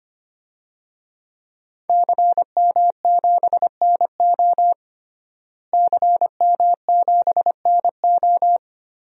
Morse code, heard as a keyed tone, about 25 wpm, 700 Hz. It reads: CM7NO CM7NO
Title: DXCC-100 Call Signs - 5 Characters - Repeated 3x - 25wpm